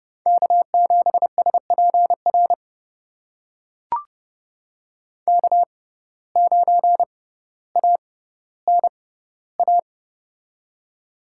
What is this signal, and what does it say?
Morse code, heard as a keyed tone, about 30 wpm, spaced as for 15 wpm, 700 Hz. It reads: K7SPR K9ANA
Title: USA Call Signs - 5 Characters - 6x Character Spacing - 30wpm